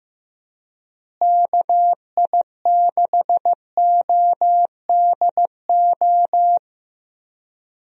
Morse code, heard as a keyed tone, about 15 wpm, 700 Hz. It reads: KI6ODO
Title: USA Call Signs - 6 Characters - Repeated 3x - 15wpm